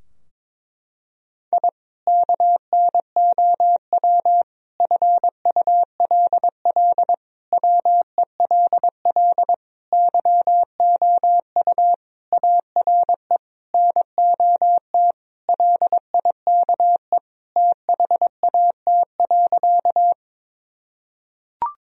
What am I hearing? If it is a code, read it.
Morse code, heard as a keyed tone, about 21 wpm, 700 Hz. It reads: I KNOW FULL WELL YOU ARE NOT LIKE THAT.
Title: Sentences from Top 1000 Words - Review Part 3 - 22wpm